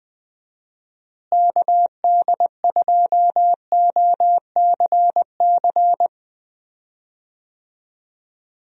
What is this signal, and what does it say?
Morse code, heard as a keyed tone, about 20 wpm, 700 Hz. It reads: KD2OCC
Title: USA Call Signs - 6 Characters - Repeated 2x - 20wpm